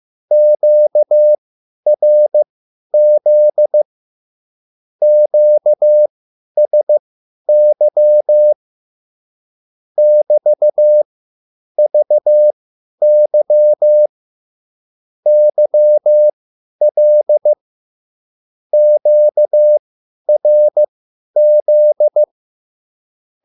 Morse code, heard as a keyed tone, about 15 wpm, spaced as for 10 wpm, 600 Hz. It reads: QRZ QSY <BT>VY YL QRZ
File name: QSL / CW lesson